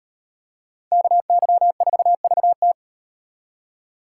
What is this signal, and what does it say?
Morse code, heard as a keyed tone, about 38 wpm, 700 Hz. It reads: KY4VT